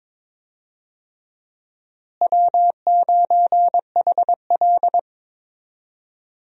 Morse code, heard as a keyed tone, about 21 wpm, 700 Hz. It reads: W9HL
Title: USA Call Signs - 4 Characters - Repeated 2x - 22wpm